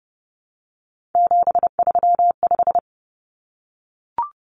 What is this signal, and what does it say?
Morse code, heard as a keyed tone, about 30 wpm, 700 Hz. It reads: 735
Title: Common Cut Numbers - 3-Digit 30wpm